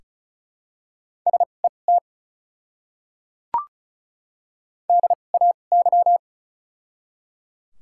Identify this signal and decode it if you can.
Morse code, heard as a keyed tone, about 34 wpm, spaced as for 25 wpm, 700 Hz. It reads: SET DAY
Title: 2x Character Spacing - Top 500 Words - 35wpm